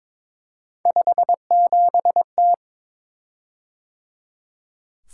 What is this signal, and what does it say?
Morse code, heard as a keyed tone, about 22 wpm, 700 Hz. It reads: 57T
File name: practice / 22wpm / Three Letter-Number Combinations - 22wpm